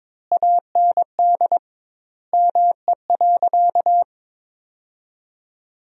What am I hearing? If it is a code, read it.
Morse code, heard as a keyed tone, about 22 wpm, 700 Hz. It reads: AND ME.
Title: 2x Word-Spacing - Sentences from Top 500 Words - 22wpm